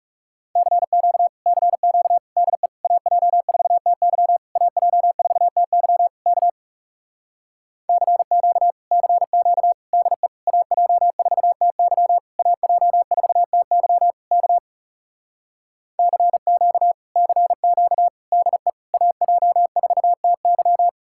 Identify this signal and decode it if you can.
Morse code, heard as a keyed tone, about 40 wpm, 700 Hz. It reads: CQ CQ DE AJ4TY AJ4TY K CQ CQ DE AJ4TY AJ4TY K CQ CQ DE AJ4TY